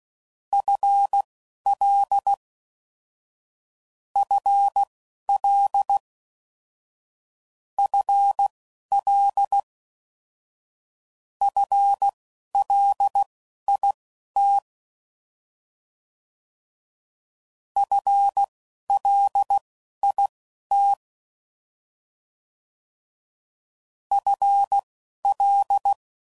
Morse code, heard as a keyed tone, about 16 wpm, spaced as for 12 wpm, 800 Hz. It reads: FL FL FL FLIT FLIT FL
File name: Prefixes / FLs & THs / FL Prefixes 3